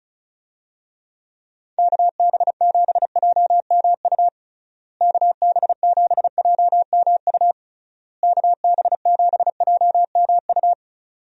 Morse code, heard as a keyed tone, about 36 wpm, 700 Hz. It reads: KB7JMU KB7JMU KB7JMU
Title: USA Call Signs - 6 Characters - Repeated 3x - 35wpm